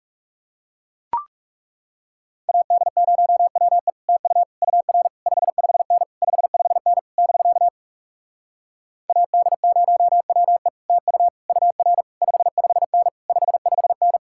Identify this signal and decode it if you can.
Morse code, heard as a keyed tone, about 40 wpm, 700 Hz. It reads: AD0WE TU UR 55N 55N <BK> AD0WE TU UR 55N 55N